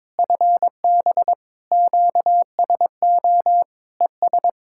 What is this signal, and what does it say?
Morse code, heard as a keyed tone, about 22 wpm, 700 Hz. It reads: FB QSO ES
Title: Practice QSO - Exchange 4 - Ending - 22wpm